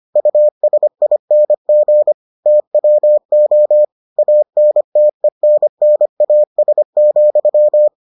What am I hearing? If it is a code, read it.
Morse code, heard as a keyed tone, about 25 wpm, 600 Hz. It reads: USING TWO ANTENNAS,